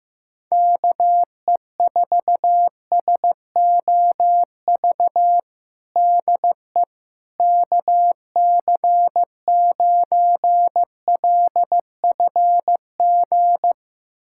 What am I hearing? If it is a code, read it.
Morse code, heard as a keyed tone, about 15 wpm, 700 Hz. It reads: KE4SOV DE KC9LFG
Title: USA Callsign DE Callsign - 6 Characters - 15wpm